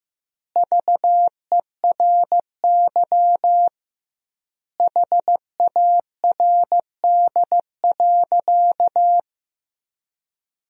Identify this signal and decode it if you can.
Morse code, heard as a keyed tone, about 16 wpm, 700 Hz. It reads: VERY HARD.
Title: 2x Word-Spacing - Sentences from Top 300 Words - 15wpm